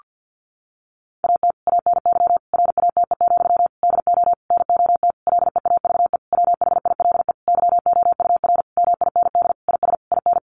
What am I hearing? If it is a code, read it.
Morse code, heard as a keyed tone, about 50 wpm, 700 Hz. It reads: AT ANY RATE, DO NOT LEAVE WHILE YOUR MIND IS IN
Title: Sentences from Top 1500 Words 50wpm